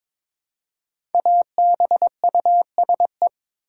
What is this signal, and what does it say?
Morse code, heard as a keyed tone, about 22 wpm, 700 Hz. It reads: ABUSE
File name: 5 Letter Words - First 3 Letters Given - 22wpm